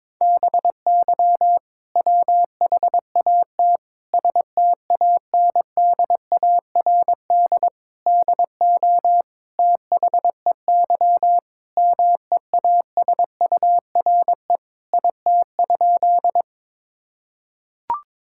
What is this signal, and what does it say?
Morse code, heard as a keyed tone, about 22 wpm, 700 Hz. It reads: BY WHAT STANDARD DO THEY MEASURE IT?
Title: Sentences from Top 1000 Words - Review Part 6 - 22wpm